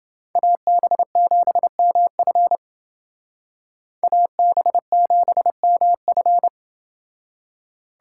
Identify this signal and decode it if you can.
Morse code, heard as a keyed tone, about 29 wpm, 700 Hz. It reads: AB7MF AB7MF